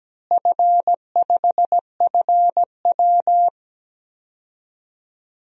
Morse code, heard as a keyed tone, about 17 wpm, 700 Hz. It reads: F5FW
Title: DXCC-100 Call Signs - 4 Characters - Repeated 3x - 17wpm